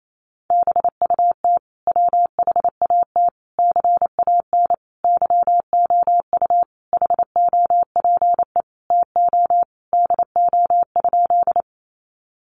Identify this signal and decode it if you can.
Morse code, heard as a keyed tone, about 28 wpm, 700 Hz. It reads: BUT WHAT CAN YOU HOPE TO DO?
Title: Sentences from Top 1000 Words 28wpm